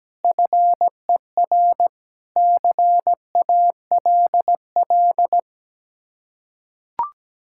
Morse code, heard as a keyed tone, about 17 wpm, 700 Hz. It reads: FER CALL
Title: Practice QSO - Exchange 1 - Beginning - 17wpm